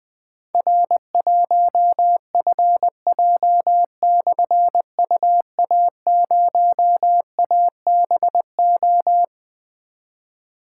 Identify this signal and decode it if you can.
Morse code, heard as a keyed tone, about 21 wpm, 700 Hz. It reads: R1FJ/UA0ABO
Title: Difficult Call Signs 20wpm